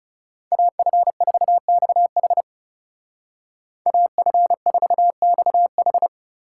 Morse code, heard as a keyed tone, about 35 wpm, 700 Hz. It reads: AF4XH AF4XH
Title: Speed Racing - USA 2x2 Call Signs 30wpm